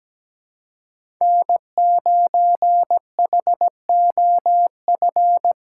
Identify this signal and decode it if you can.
Morse code, heard as a keyed tone, about 17 wpm, 700 Hz. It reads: N9HOF
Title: USA Call Signs - 6x Character Spacing - 17wpm